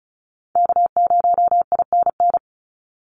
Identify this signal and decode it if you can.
Morse code, heard as a keyed tone, about 34 wpm, 700 Hz. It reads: K0INN
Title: Call Signs Repeated 2x - 35wpm